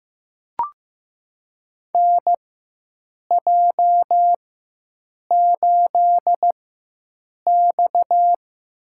Morse code, heard as a keyed tone, about 15 wpm, spaced as for 7 wpm, 700 Hz. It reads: NJ8X
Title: USA Call Signs - 4 Characters - 4x Character Spacing - 15wpm